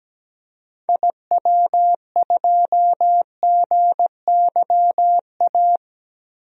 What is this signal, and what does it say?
Morse code, heard as a keyed tone, about 17 wpm, 700 Hz. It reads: IW2GYA
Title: DXCC-100 Call Signs - 6 Characters - Repeated 2x - 17wpm